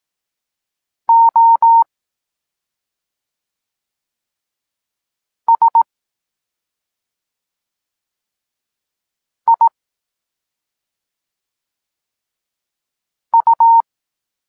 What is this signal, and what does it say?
Morse code, heard as a keyed tone, about 18 wpm, spaced as for 2 wpm, 920 Hz. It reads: OSIU
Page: Tekst 8 – doskonalenie odbioru poznanych liter